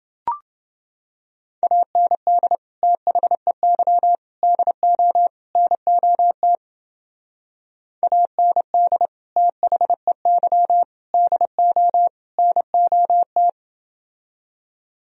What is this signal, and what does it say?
Morse code, heard as a keyed tone, about 28 wpm, 700 Hz. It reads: AND THEY DO NOT AND THEY DO NOT